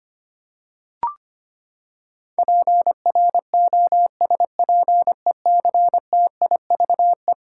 Morse code, heard as a keyed tone, about 25 wpm, 700 Hz. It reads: PROSPECTIVE
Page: morse-code-ninja-practice-25wpm